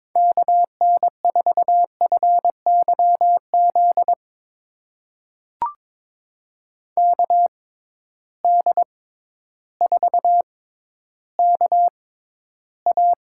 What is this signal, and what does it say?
Morse code, heard as a keyed tone, about 21 wpm, spaced as for 12 wpm, 700 Hz. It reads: KN4FYZ KD4KA
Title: USA Call Signs - 6 Characters - 6x Character Spacing - 22wpm